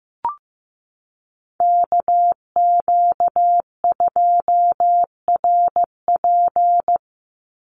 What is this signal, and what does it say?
Morse code, heard as a keyed tone, about 15 wpm, 700 Hz. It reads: KQ2RP
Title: NA SOTA Chaser Callsigns 15wpm